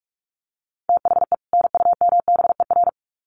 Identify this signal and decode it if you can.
Morse code, heard as a keyed tone, about 46 wpm, 700 Hz. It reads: THE NUMBER
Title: Sets of 2 Words 45wpm